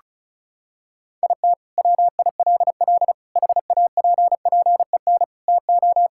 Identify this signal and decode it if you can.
Morse code, heard as a keyed tone, about 36 wpm, 700 Hz. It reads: IT WILL HAPPEN TO
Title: Sentences from Top 1000 Words - Review Part 6 - 35wpm